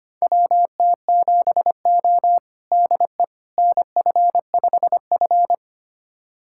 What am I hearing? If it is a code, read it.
Morse code, heard as a keyed tone, about 25 wpm, 700 Hz. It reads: WT7O DE NF5F
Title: USA Callsign DE Callsign - 4 Characters - 25wpm